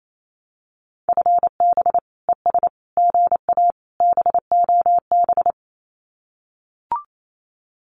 Morse code, heard as a keyed tone, about 28 wpm, 700 Hz. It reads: FB ES GA BOB
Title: Common QSO Phrases Encore 28wpm